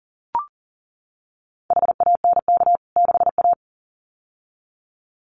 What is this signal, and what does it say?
Morse code, heard as a keyed tone, about 40 wpm, 700 Hz. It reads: HANK 6A
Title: CWT Exchange 40wpm